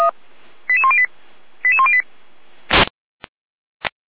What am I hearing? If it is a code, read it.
Five-tone selective call (ZVEI), heard as quick runs of five tones, tones of about 70 ms. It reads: 88108 88108